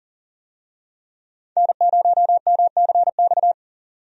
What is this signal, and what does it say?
Morse code, heard as a keyed tone, about 37 wpm, 700 Hz. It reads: N0MCX
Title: USA Call Signs - 5 Characters - 40wpm